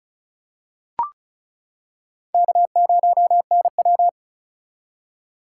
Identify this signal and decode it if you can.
Morse code, heard as a keyed tone, about 34 wpm, 700 Hz. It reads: K0NW
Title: K1USN SST QSO Practice - 35wpm